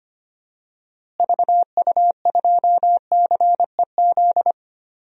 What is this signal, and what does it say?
Morse code, heard as a keyed tone, about 25 wpm, 700 Hz. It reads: VU2CEZ
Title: DXCC-100 Call Signs - 6 Characters - Repeated 2x - 25wpm